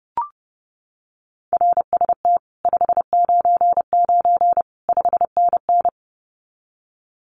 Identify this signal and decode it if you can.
Morse code, heard as a keyed tone, about 30 wpm, 700 Hz. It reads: RST 599 5NN
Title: Common QSO Phrases Encore 30wpm